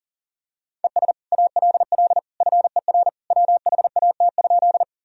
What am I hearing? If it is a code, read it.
Morse code, heard as a keyed tone, about 40 wpm, 700 Hz. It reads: ES ALL FER WHAT?
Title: Sentences from Top 100 Words and Abbreviations 40wpm